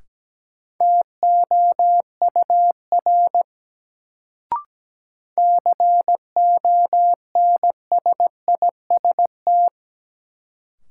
Morse code, heard as a keyed tone, about 17 wpm, 700 Hz. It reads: TOUR CONSIST
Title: Sentences from Top 1400 Words - New Word Drill Alt - 17wpm